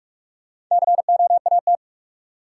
Morse code, heard as a keyed tone, about 46 wpm, 700 Hz. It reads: COAT
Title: Sentences from Top 1600 Words - New Word Drill Alt - 45wpm